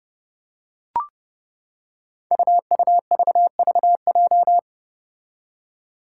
Morse code, heard as a keyed tone, about 28 wpm, 700 Hz. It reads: UUVVJ
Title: Five Challenging Letter-Number Combinations 30wpm